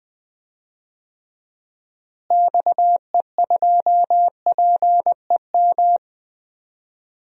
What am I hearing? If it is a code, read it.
Morse code, heard as a keyed tone, about 20 wpm, 700 Hz. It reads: XE2PEM
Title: DXCC-100 Call Signs - 6 Characters - Repeated 3x - 20wpm